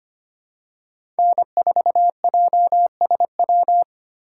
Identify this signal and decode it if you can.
Morse code, heard as a keyed tone, about 25 wpm, 700 Hz. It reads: N4JSW
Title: USA Call Signs - 2x Character Spacing - 25wpm